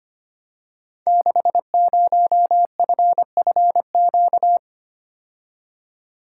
Morse code, heard as a keyed tone, about 25 wpm, 700 Hz. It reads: 60FFQ
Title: Code Groups - 5 Letter:Numbers - 25wpm